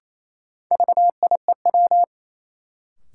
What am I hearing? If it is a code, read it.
Morse code, heard as a keyed tone, about 28 wpm, 700 Hz. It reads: VIEW
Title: Sentences from Top 900 Words - New Word Drill - 28wpm